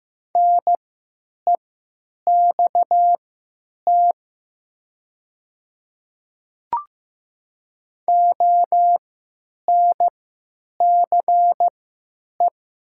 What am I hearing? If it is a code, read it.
Morse code, heard as a keyed tone, about 15 wpm, spaced as for 9 wpm, 700 Hz. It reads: NEXT ONCE
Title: 3x Character Spacing - Top 500 Words - 15wpm